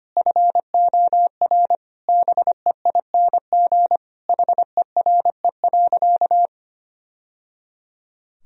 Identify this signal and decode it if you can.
Morse code, heard as a keyed tone, about 25 wpm, 700 Hz. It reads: FOR BEING HERE.
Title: Sentences from Top 1000 Words - Review Part 3 - 25wpm